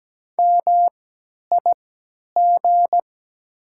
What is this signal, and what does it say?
Morse code, heard as a keyed tone, about 17 wpm, spaced as for 10 wpm, 700 Hz. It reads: MIG
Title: 3x Character Spacing - Top 300 Words - 17wpm